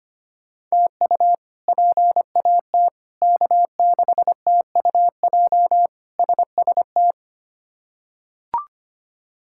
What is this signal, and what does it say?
Morse code, heard as a keyed tone, about 25 wpm, 700 Hz. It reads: TU PAT K6TUJ SST
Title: K1USN SST QSO Practice - 25wpm